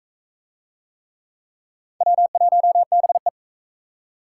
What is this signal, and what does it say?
Morse code, heard as a keyed tone, about 41 wpm, 700 Hz. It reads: W1BE